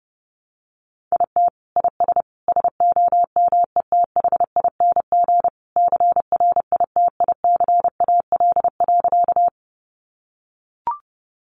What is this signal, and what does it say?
Morse code, heard as a keyed tone, about 30 wpm, 700 Hz. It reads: IT IS SOMETHING CRITICAL.
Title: Sentences from Top 1100 Words 30wpm